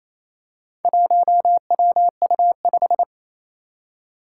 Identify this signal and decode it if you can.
Morse code, heard as a keyed tone, about 27 wpm, 700 Hz. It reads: 1WU5
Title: Four Challenging Letter-Number Combinations 28wpm